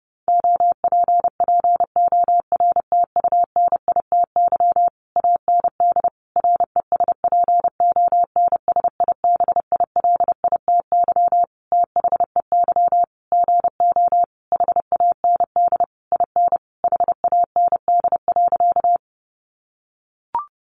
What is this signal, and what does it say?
Morse code, heard as a keyed tone, about 30 wpm, 700 Hz. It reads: OPPORTUNITY AND RESPONSIBILITY THEY GO HAND IN HAND.
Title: Sentences from Top 1200 Words 30wpm